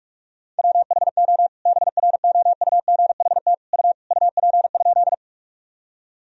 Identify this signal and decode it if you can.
Morse code, heard as a keyed tone, about 46 wpm, 700 Hz. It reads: WHO BROUGHT U UP?
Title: Sentences from Top 500 Words and Abbreviations 45wpm